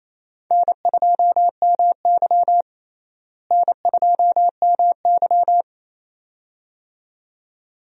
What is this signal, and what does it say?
Morse code, heard as a keyed tone, about 28 wpm, 700 Hz. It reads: N2MY N2MY
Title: USA Call Signs - 4 Characters - Repeated 2x - 28wpm